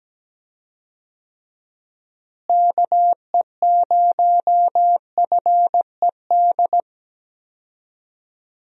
Morse code, heard as a keyed tone, about 17 wpm, 700 Hz. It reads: KE0FED
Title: USA Call Signs - 6 Characters - Repeated 3x - 17wpm